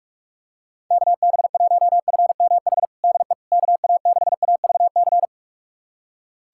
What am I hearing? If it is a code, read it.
Morse code, heard as a keyed tone, about 46 wpm, 700 Hz. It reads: KB1FMH DE KA6AVC
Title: USA Callsign DE Callsign - 6 Characters - 45wpm